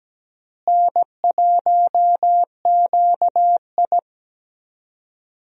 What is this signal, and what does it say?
Morse code, heard as a keyed tone, about 17 wpm, 700 Hz. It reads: N1QI
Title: USA Call Signs - 4 Characters - Repeated 3x - 17wpm